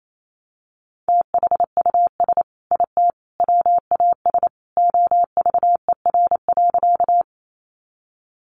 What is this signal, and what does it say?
Morse code, heard as a keyed tone, about 28 wpm, 700 Hz. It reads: THUS IT WAS OVER.
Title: Sentences from Top 800 Words 28wpm